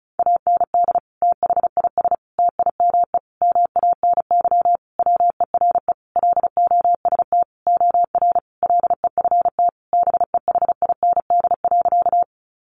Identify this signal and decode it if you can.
Morse code, heard as a keyed tone, about 34 wpm, 700 Hz. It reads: AND THIS TIME MANY WERE LOST OR LEFT BEHIND.
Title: Sentences from Top 1400 Words 35wpm